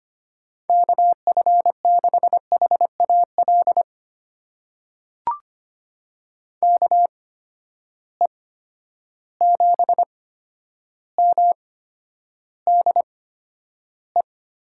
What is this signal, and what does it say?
Morse code, heard as a keyed tone, about 25 wpm, spaced as for 11 wpm, 700 Hz. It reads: KF6HAL KE7MDE
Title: USA Call Signs - 6 Characters - 8x Character Spacing - 25wpm